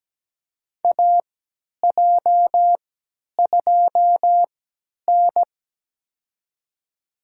Morse code, heard as a keyed tone, about 17 wpm, spaced as for 10 wpm, 700 Hz. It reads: AJ2N
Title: USA Call Signs - 4 Characters - 3x Character Spacing - 17wpm